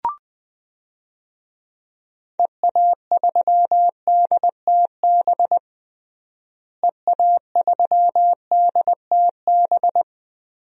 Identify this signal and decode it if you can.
Morse code, heard as a keyed tone, about 20 wpm, 700 Hz. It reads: EA3DTB EA3DTB
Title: DXCC-100 Call Signs - 6 Characters - Repeated 3x - 20wpm